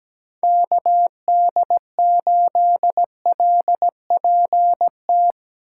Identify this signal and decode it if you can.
Morse code, heard as a keyed tone, about 17 wpm, 700 Hz. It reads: KD8LPT
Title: USA Call Signs - 6 Characters - 2x Character Spacing - 17wpm